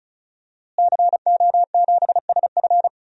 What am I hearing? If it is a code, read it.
Morse code, heard as a keyed tone, about 36 wpm, 700 Hz. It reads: CO7SF
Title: DXCC-100 Call Signs - 5 Characters - Repeated 2x - 35wpm